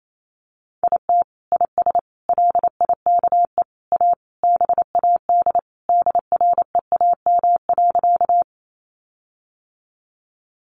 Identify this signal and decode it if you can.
Morse code, heard as a keyed tone, about 28 wpm, 700 Hz. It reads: IT IS LIKE A BAD DREAM.
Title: Sentences from Top 1500 Words 28wpm